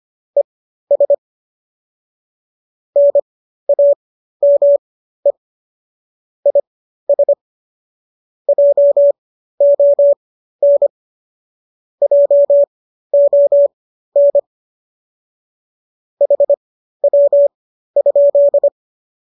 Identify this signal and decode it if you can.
Morse code, heard as a keyed tone, about 25 wpm, spaced as for 13 wpm, 570 Hz. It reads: ES NAME IS JON JON HW?
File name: qso207 13